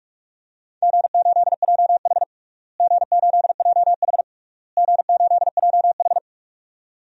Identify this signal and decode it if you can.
Morse code, heard as a keyed tone, about 46 wpm, 700 Hz. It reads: G8JH G8JH G8JH
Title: DXCC-100 Call Signs - 4 Characters - Repeated 3x - 45wpm